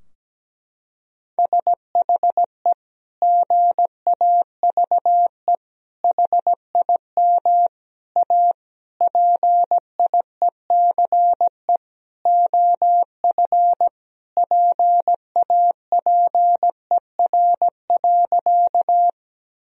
Morse code, heard as a keyed tone, about 17 wpm, 700 Hz. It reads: SHE GAVE HIM A PIECE OF PAPER.
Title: Sentences from Top 1000 Words - Review Part 2 - 17wpm